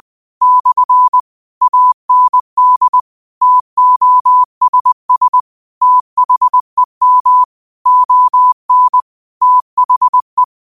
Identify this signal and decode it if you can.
Morse code, heard as a keyed tone, about 20 wpm, 1000 Hz. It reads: / AND TOSS THEM ON THE
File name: qotdfiles / 20wpm